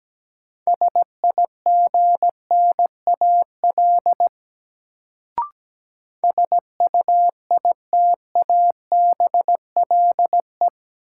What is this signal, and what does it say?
Morse code, heard as a keyed tone, about 17 wpm, 700 Hz. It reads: SIGNAL SUITABLE
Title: Sentences from Top 1400 Words - New Word Drill Alt - 17wpm